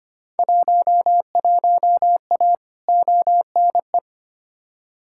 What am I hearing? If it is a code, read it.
Morse code, heard as a keyed tone, about 25 wpm, 700 Hz. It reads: 11A ONE
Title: Field Day Exchanges Encore - 25wpm